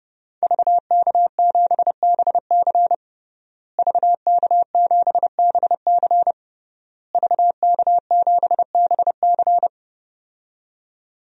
Morse code, heard as a keyed tone, about 30 wpm, 700 Hz. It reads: VK7BC VK7BC VK7BC
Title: DXCC-100 Call Signs - 5 Characters - Repeated 3x - 30wpm